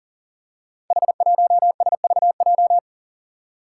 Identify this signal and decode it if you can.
Morse code, heard as a keyed tone, about 40 wpm, 700 Hz. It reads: H1SVJ
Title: Five Challenging Letter-Number Combinations 40wpm